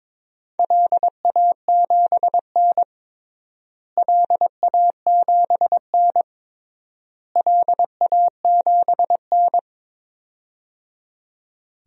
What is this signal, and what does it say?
Morse code, heard as a keyed tone, about 22 wpm, 700 Hz. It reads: LA7N LA7N LA7N
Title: DXCC-100 Call Signs - 4 Characters - Repeated 3x - 22wpm